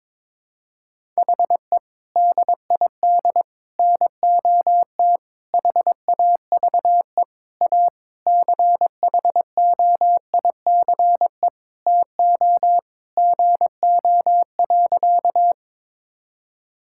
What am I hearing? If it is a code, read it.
Morse code, heard as a keyed tone, about 21 wpm, 700 Hz. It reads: HE DID NOT HAVE A CHOICE TO GO.
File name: practice / 22wpm / Sentences with 4-Word Phrases - 22wpm